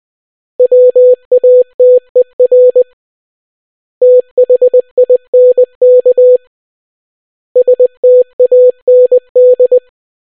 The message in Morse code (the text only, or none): WATER THINK STAND